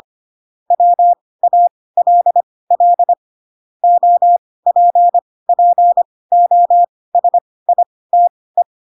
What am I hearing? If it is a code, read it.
Morse code, heard as a keyed tone, about 25 wpm, spaced as for 18 wpm, 700 Hz. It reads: WALL OPPOSITE
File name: Pinocchio ch3 12 25 ik1hge